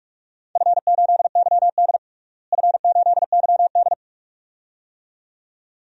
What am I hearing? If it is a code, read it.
Morse code, heard as a keyed tone, about 46 wpm, 700 Hz. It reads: F8YD F8YD
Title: DXCC-100 Call Signs - 4 Characters - Repeated 2x - 45wpm